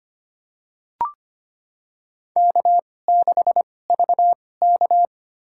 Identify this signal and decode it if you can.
Morse code, heard as a keyed tone, about 25 wpm, spaced as for 18 wpm, 700 Hz. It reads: K6VK
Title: USA Call Signs - 4 Characters - 2x Character Spacing - 25wpm